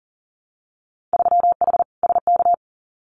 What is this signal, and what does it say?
Morse code, heard as a keyed tone, about 40 wpm, 700 Hz. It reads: 3H SK
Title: Winter Field Day Exchanges 40wpm